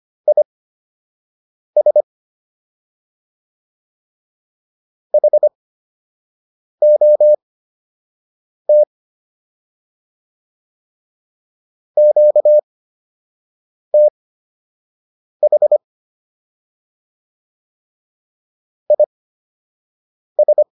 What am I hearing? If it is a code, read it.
Morse code, heard as a keyed tone, about 25 wpm, spaced as for 6 wpm, 600 Hz. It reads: IS HOT QTH IS